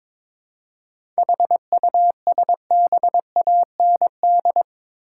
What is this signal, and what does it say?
Morse code, heard as a keyed tone, about 22 wpm, 700 Hz. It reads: HUSBAND
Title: Seven Letter Words - 22wpm